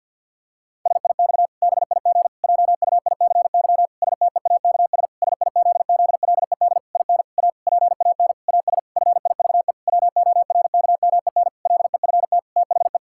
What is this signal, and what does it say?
Morse code, heard as a keyed tone, about 50 wpm, 700 Hz. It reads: SIX BIG JUICY STEAKS SIZZLED IN A PAN AS FIVE WORKMEN LEFT THE
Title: Pangram Challenge v2 - 50wpm